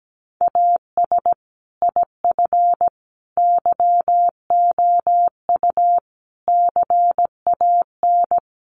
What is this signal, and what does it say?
Morse code, heard as a keyed tone, about 17 wpm, 700 Hz. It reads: AS IF YOU CAN
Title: Sentences from Top 1300 Words 17wpm